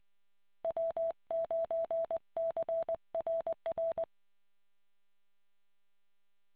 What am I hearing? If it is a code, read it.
Morse code, heard as a keyed tone, about 20 wpm, 660 Hz. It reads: W9CRR